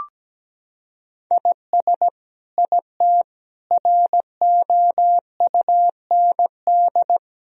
Morse code, heard as a keyed tone, about 17 wpm, 700 Hz. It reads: IS IT ROUND
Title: Sentences from Top 1000 Words - Review Part 3 - 17wpm